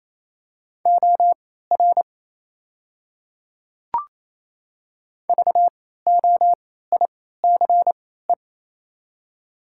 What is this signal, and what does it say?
Morse code, heard as a keyed tone, about 28 wpm, spaced as for 16 wpm, 700 Hz. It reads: OR VOICE
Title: 3x Character Spacing - Top 500 Words - 28wpm